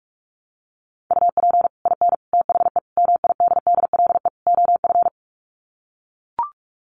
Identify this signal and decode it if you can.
Morse code, heard as a keyed tone, about 44 wpm, 700 Hz. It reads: UP IN THE MIDDLE OF
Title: Sets of 5 Words Encore 45wpm